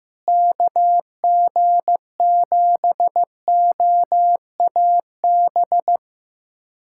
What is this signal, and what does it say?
Morse code, heard as a keyed tone, about 16 wpm, 700 Hz. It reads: KG7OAB
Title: USA Callsign DE Callsign - 6 Characters - 15wpm